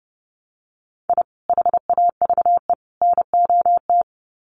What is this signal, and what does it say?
Morse code, heard as a keyed tone, about 30 wpm, 700 Hz. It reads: I HAVE NOT
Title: Sets of 3 Words 30wpm